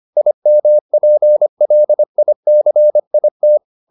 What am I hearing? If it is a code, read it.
Morse code, heard as a keyed tone, about 25 wpm, 600 Hz. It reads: IMPLICIT